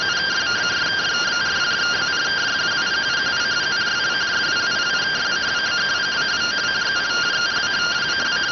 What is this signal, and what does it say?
Two-tone teleprinter signal, about 170 Hz shift.